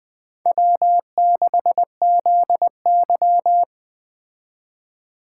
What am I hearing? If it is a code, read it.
Morse code, heard as a keyed tone, about 20 wpm, 700 Hz. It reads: W6ZY
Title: Speed Racing - USA 1x2 Call Signs 17wpm